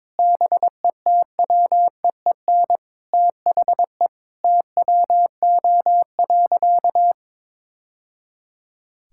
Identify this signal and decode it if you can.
Morse code, heard as a keyed tone, about 22 wpm, 700 Hz. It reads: BETWEEN THE TWO.
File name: practice / 22wpm / Sentences with 2-Word Phrases - 22wpm